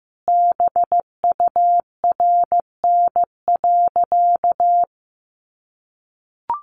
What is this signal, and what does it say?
Morse code, heard as a keyed tone, about 15 wpm, 700 Hz. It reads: BURN.
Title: Sentences from Top 1300 Words 15wpm